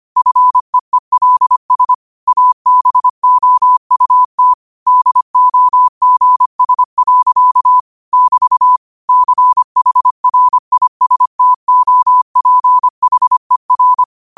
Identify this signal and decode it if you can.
Morse code, heard as a keyed tone, about 25 wpm, 1000 Hz. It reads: FEELS ABOUT DOGS. <BT> CHRISTOPHER
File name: qotdfiles / 25wpm